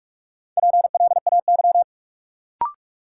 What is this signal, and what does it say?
Morse code, heard as a keyed tone, about 45 wpm, 700 Hz. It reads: PLAY
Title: Sentences from Top 200 Words - New Word Drill - 45wpm